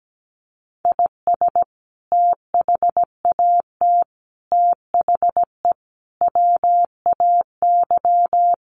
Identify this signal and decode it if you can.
Morse code, heard as a keyed tone, about 17 wpm, 700 Hz. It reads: IS THAT THE WAY
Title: Sentences from Top 1300 Words 17wpm